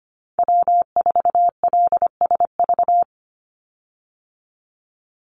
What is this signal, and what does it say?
Morse code, heard as a keyed tone, about 25 wpm, 700 Hz. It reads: W4LSV
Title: Call Signs Repeated 3x - 25wpm